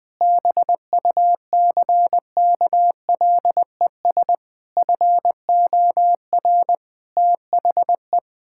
Morse code, heard as a keyed tone, about 20 wpm, 700 Hz. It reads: BUCKLES FOR THE
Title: Pangram Challenge v2 - 20wpm